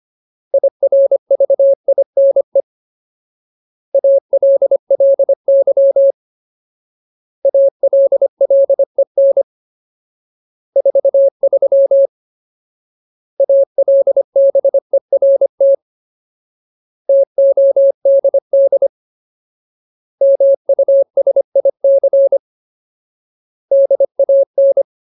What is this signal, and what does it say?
Morse code, heard as a keyed tone, about 25 wpm, 550 Hz. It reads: IRVINE ALLY ALLEN 43 ALBERT TODD MUSIC DAN
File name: Ultimate Word List / 25WPM Ultimate